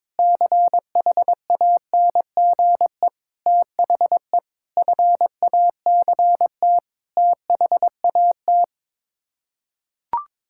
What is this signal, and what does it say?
Morse code, heard as a keyed tone, about 22 wpm, 700 Hz. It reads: CHANGE THE FACT THAT
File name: practice / 22wpm / Sets of 7 Words - 22wpm